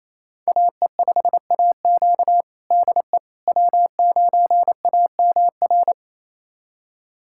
Morse code, heard as a keyed tone, about 28 wpm, 700 Hz. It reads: AE5AQ DE W9AMR
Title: USA Callsign DE Callsign - 5 Characters - 28wpm